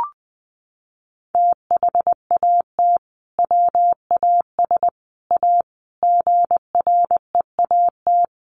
Morse code, heard as a keyed tone, about 20 wpm, 700 Hz. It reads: THAT WAS A GREAT
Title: Sentences from Top 1000 Words 20wpm